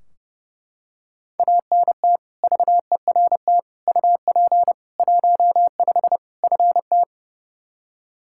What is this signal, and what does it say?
Morse code, heard as a keyed tone, about 30 wpm, 700 Hz. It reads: ANT VERT UP 15 FT
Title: K1USN SST QSO Practice 30wpm